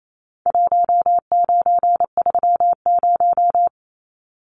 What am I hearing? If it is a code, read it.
Morse code, heard as a keyed tone, about 28 wpm, 700 Hz. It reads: 1930
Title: Numbers - Contest Style - 4-Digit 28wpm